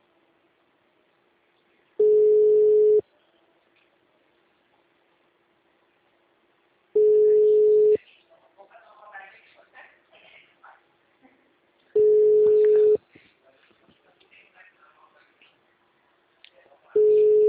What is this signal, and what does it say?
Ringback tone (Europe, 425 Hz), with speech.
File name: out- -2015- - - .93894